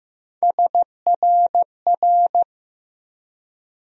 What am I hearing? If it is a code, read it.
Morse code, heard as a keyed tone, about 15 wpm, 700 Hz. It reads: SRR
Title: USA Call Sign Postfixes - 3 Letters 15wpm